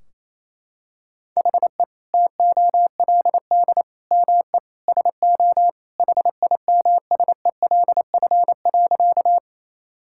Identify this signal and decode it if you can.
Morse code, heard as a keyed tone, about 28 wpm, 700 Hz. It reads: HE TOLD ME SO HIMSELF.
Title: Sentences from Top 1600 Words 28wpm